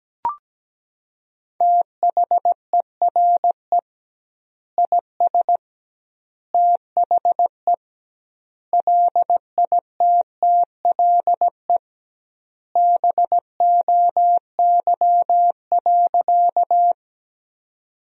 Morse code, heard as a keyed tone, about 17 wpm, 700 Hz. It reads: THERE IS THE LITTLE BOY.
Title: 2x Word-Spacing - Sentences from Top 200 Words - 17wpm